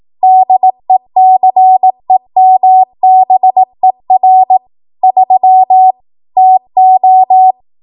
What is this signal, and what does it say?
Morse code, heard as a keyed tone, about 18 wpm, 750 Hz. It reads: DECEMBER 3 TO
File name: Archive / 18 WPM / 18WPM